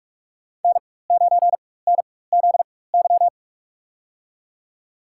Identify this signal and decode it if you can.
Morse code, heard as a keyed tone, about 45 wpm, spaced as for 21 wpm, 700 Hz. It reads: N9NZY
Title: USA Call Signs - 5 Characters - 4x Character Spacing - 45wpm